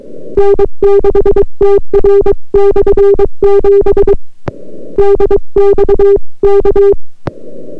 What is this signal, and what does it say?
Morse code, heard as a keyed tone, about 23 wpm, 400 Hz. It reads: N6TR/7 DX K